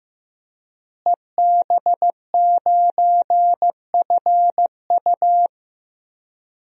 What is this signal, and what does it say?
Morse code, heard as a keyed tone, about 16 wpm, 700 Hz. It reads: EB9FU
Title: DXCC-100 Call Signs - 5 Characters - Repeated 3x - 15wpm